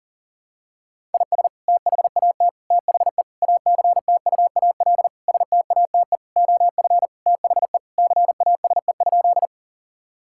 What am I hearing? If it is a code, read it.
Morse code, heard as a keyed tone, about 40 wpm, 700 Hz. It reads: IS THAT THE ACTUAL STATE OF THE CASE?
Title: Sentences from Top 1000 Words - Review Part 5 - 40wpm